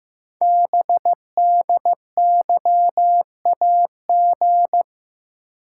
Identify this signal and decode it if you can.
Morse code, heard as a keyed tone, about 15 wpm, 700 Hz. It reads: BDYAG
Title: Code Groups - 5 Letters - 15wpm